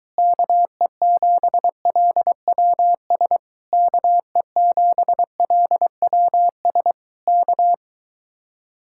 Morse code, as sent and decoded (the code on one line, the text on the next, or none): -.- . --... .-.. .-- ... / -.- . --... .-.. .-- ... / -.-
KE7LWS KE7LWS K